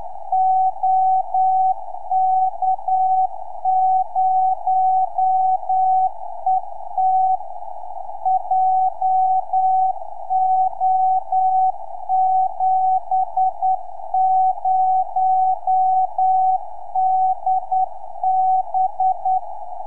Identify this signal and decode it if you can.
Morse code, heard as a keyed tone, about 9 wpm, 750 Hz. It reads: OK0ET JO70DB
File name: ok0et 3cm